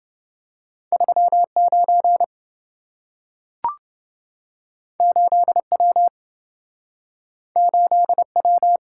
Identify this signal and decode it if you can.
Morse code, heard as a keyed tone, about 27 wpm, 700 Hz. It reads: 39 8W 8W